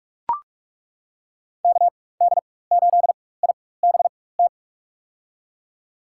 Morse code, heard as a keyed tone, about 42 wpm, spaced as for 20 wpm, 700 Hz. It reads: KD8IBT
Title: USA Call Signs - 4x Character Spacing - 45wpm